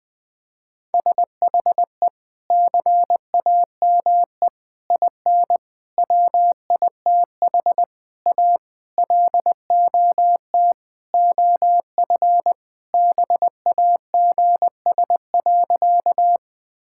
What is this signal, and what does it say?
Morse code, heard as a keyed tone, about 21 wpm, 700 Hz. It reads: SHE CAME IN WITH A LOT OF BAGS.
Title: Sentences with 4-Word Phrases - 20wpm